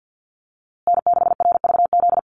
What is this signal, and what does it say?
Morse code, heard as a keyed tone, about 50 wpm, 700 Hz. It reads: N6RVZ
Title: USA Call Signs Repeated 3x 50wpm